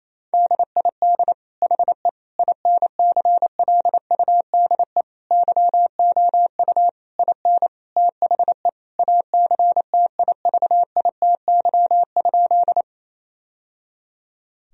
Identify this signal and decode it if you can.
Morse code, heard as a keyed tone, about 28 wpm, 700 Hz. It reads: DID HE INCLUDE YOU IN THE ACTIVITY?
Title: Sentences from Top 1000 Words - Review Part 4 - 28wpm